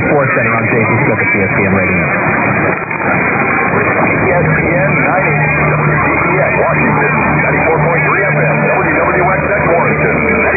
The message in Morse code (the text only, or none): none